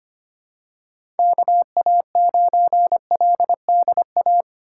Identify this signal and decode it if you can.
Morse code, heard as a keyed tone, about 25 wpm, 700 Hz. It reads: KA9LDA
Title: USA Call Signs - 6 Characters - 3x Character Spacing - 25wpm